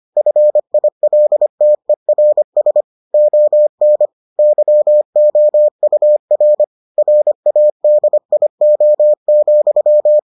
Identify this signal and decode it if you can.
Morse code, heard as a keyed tone, about 25 wpm, 600 Hz. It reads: FILTERS ON YOUR RADIO,